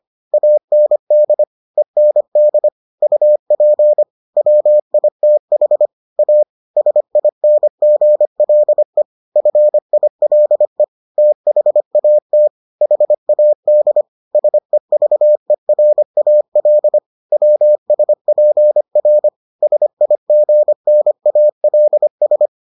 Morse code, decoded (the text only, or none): AND END UP WITH A SINGLE FILE THAT HAD SEVERAL WSPR SIGNALS